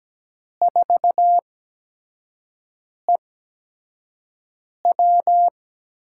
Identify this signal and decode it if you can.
Morse code, heard as a keyed tone, about 17 wpm, spaced as for 5 wpm, 700 Hz. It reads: 4EW
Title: USA Call Signs - 6 Characters - 8x Character Spacing - 17wpm